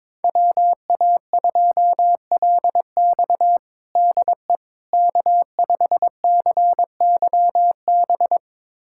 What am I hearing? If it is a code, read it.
Morse code, heard as a keyed tone, about 22 wpm, 700 Hz. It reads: WA2LX DE K5CYB
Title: USA Callsign DE Callsign - 5 Characters - 22wpm